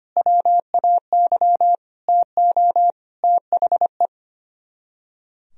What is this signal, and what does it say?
Morse code, heard as a keyed tone, about 31 wpm, 700 Hz. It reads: WAY TO THE
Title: Speed-Racing - Sets of 5 Words 25wpm